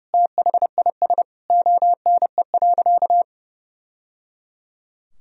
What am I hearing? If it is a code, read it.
Morse code, heard as a keyed tone, about 30 wpm, 700 Hz. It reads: THIS ONE.
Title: Sentences from Top 1000 Words - Review Part 4 - 30wpm